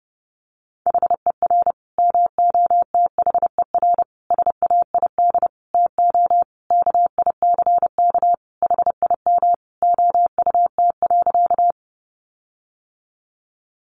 Morse code, heard as a keyed tone, about 30 wpm, 700 Hz. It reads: HER MOTHER SAID TO KICK HIM OUT.
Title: Sentences from Top 1300 Words 30wpm